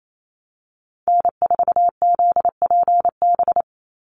Morse code, heard as a keyed tone, about 28 wpm, 700 Hz. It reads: N4ZPB
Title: USA Call Signs 28wpm